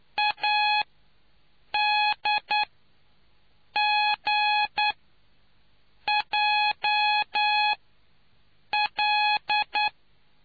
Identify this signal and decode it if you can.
Morse code, heard as a keyed tone, about 9 wpm, spaced as for 6 wpm, 840 Hz. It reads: ADGJL